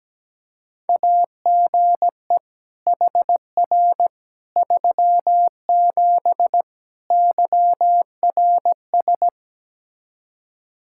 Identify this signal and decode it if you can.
Morse code, heard as a keyed tone, about 17 wpm, 700 Hz. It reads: AGE HR 37 YRS
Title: Practice QSO - Exchange 3 - More Info - 17wpm